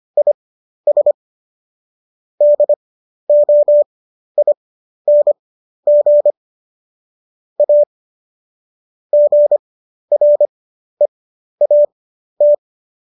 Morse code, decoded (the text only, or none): IS DOING A GREAT